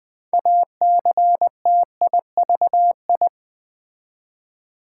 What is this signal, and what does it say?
Morse code, heard as a keyed tone, about 20 wpm, 700 Hz. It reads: ACTIVI
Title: Build A Word - 2 Letters At A Time - 8-Letter Words - 20wpm